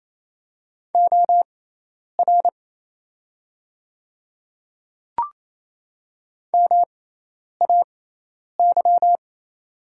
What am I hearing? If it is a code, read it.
Morse code, heard as a keyed tone, about 28 wpm, spaced as for 10 wpm, 700 Hz. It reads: OR MAY
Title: 6x Character Spacing - Top 100 Words - 28wpm